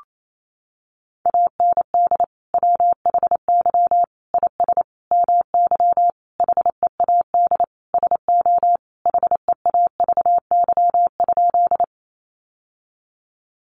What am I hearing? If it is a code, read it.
Morse code, heard as a keyed tone, about 28 wpm, 700 Hz. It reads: AND WHY IS MY HEAD SO HEAVY?
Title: Sentences from Top 1100 Words 28wpm